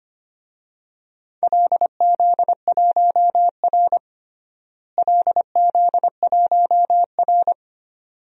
Morse code, heard as a keyed tone, about 25 wpm, 700 Hz. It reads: LZ1R LZ1R
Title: DXCC-100 Call Signs - 4 Characters - Repeated 2x - 25wpm